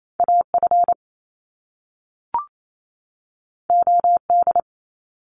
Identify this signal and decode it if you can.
Morse code, heard as a keyed tone, about 28 wpm, 700 Hz. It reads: AF OD
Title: Top Two Letter Combinations in Words 28wpm